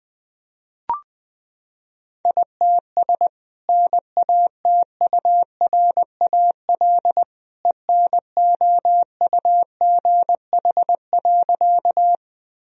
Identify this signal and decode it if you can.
Morse code, heard as a keyed tone, about 20 wpm, 700 Hz. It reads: ITS NATURAL ENOUGH.
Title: Sentences from Top 1000 Words - Review Part 4 - 20wpm